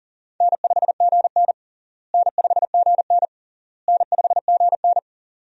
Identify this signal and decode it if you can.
Morse code, heard as a keyed tone, about 40 wpm, 700 Hz. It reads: N5GN N5GN N5GN
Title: USA Call Signs - 4 Characters - Repeated 3x - 40wpm